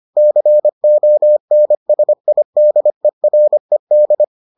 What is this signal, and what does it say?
Morse code, heard as a keyed tone, about 25 wpm, 600 Hz. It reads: CONSIDERED